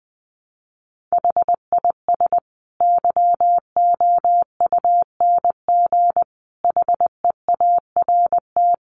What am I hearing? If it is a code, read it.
Morse code, heard as a keyed tone, about 19 wpm, 700 Hz. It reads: HIS YOUNG HEART
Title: Sentences from Top 1300 Words 20wpm